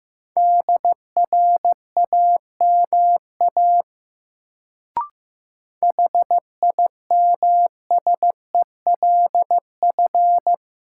Morse code, heard as a keyed tone, about 15 wpm, 700 Hz. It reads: DRAMA HIMSELF
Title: Sentences from Top 1500 Words - New Word Drill - 15wpm